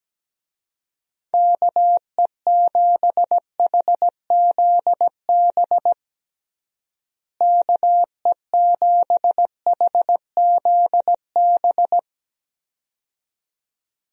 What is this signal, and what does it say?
Morse code, heard as a keyed tone, about 17 wpm, 700 Hz. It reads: KE7HZB KE7HZB
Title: USA Call Signs - 6 Characters - Repeated 2x - 17wpm